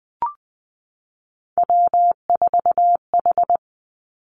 Morse code, heard as a keyed tone, about 19 wpm, 700 Hz. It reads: W4H
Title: Three Challenging Letter-Number Combinations 20wpm